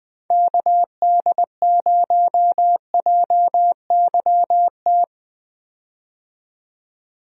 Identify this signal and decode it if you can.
Morse code, heard as a keyed tone, about 20 wpm, 700 Hz. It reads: KD0JYT
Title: USA Call Signs - 6 Characters - 20wpm